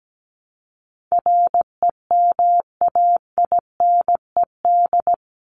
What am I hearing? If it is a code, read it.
Morse code, heard as a keyed tone, about 17 wpm, 700 Hz. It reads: REMAINED
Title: ED Suffix - Top 100 Words with Base Words 17wpm